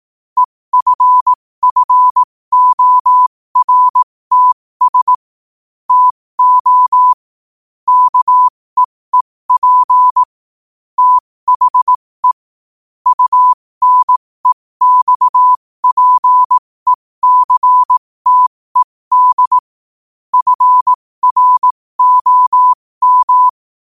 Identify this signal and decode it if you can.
Morse code, heard as a keyed tone, about 18 wpm, spaced as for 15 wpm, 1000 Hz. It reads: EFFORTS TO KEEP THE UNEXPECTED FROM
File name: qotdfiles / 15wpm